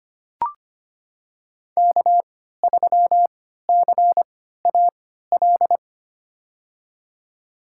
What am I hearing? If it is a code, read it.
Morse code, heard as a keyed tone, about 25 wpm, spaced as for 14 wpm, 700 Hz. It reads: K3CAL
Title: USA Call Signs - 3x Character Spacing - 25wpm